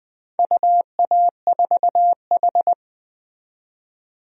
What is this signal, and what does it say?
Morse code, heard as a keyed tone, about 21 wpm, 700 Hz. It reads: UA4H
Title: DXCC-100 Call Signs - 4 Characters - 20wpm